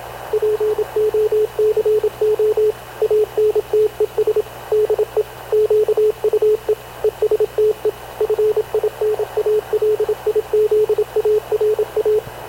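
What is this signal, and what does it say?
Morse code, heard as a keyed tone, about 27 wpm, 430 Hz, with radio noise. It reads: POCO ANTES DE QUE ESTE FINALIZARA
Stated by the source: Havana Radio / CLA sending Spanish press (PX) in Morse, recorded by Radio Officer Paul Dumesnil onboard CNAV QUEST/CZDO.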